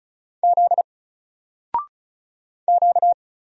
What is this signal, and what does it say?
Morse code, heard as a keyed tone, about 35 wpm, 700 Hz. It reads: Z Q
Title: Single Letter-Number 35wpm